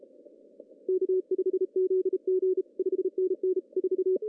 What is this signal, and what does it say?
Morse code, heard as a keyed tone, about 34 wpm, 380 Hz. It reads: K5ZG 5NN 4
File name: CW / cqww-cw-k5tr